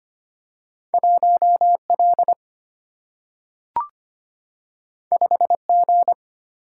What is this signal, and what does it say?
Morse code, heard as a keyed tone, about 25 wpm, 700 Hz. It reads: 1 L 5 G
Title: Single-Letter Single-Number 25wpm